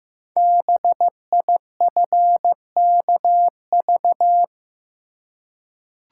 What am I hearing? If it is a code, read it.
Morse code, heard as a keyed tone, about 15 wpm, 700 Hz. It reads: BIFKV
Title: Code Groups - 5 Letter:Numbers - 15wpm